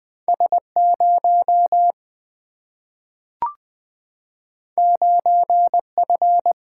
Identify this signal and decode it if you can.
Morse code, heard as a keyed tone, about 20 wpm, 700 Hz. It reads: S0 9F
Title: Single-Letter Single-Number 20wpm